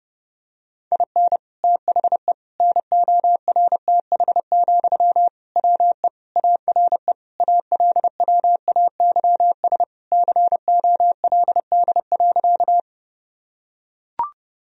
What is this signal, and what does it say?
Morse code, heard as a keyed tone, about 30 wpm, 700 Hz. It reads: IN THE NORTH, WE ARE ALWAYS COLD.
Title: Sentences from Top 1000 Words - Review Part 1 - 30wpm